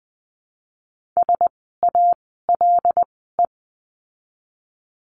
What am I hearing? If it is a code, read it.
Morse code, heard as a keyed tone, about 20 wpm, 700 Hz. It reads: S A L E
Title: 075 Lesson 13f - Letter L - Words 20wpm